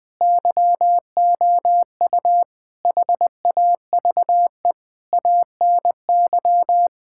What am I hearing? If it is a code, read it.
Morse code, heard as a keyed tone, about 21 wpm, 700 Hz. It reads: YOU HAVE ANY
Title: Sets of 8 Words - 20wpm